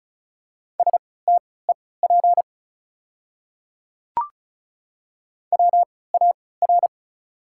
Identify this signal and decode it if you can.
Morse code, heard as a keyed tone, about 34 wpm, spaced as for 20 wpm, 690 Hz. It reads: STEP WAR
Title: 3x Character Spacing - Top 500 Words - 35wpm